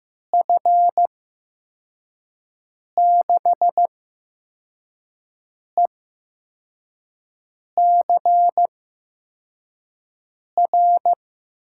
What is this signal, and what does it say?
Morse code, heard as a keyed tone, about 15 wpm, spaced as for 4 wpm, 700 Hz. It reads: F6ECR
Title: USA Call Signs - 6 Characters - 8x Character Spacing - 15wpm